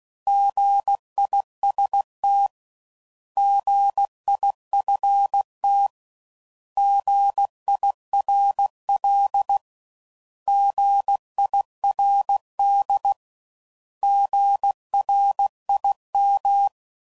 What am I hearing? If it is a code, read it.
Morse code, heard as a keyed tone, about 16 wpm, 800 Hz. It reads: GIST GIFT GIRL GIRD GRIM
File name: Prefixes / By First Letter / Gs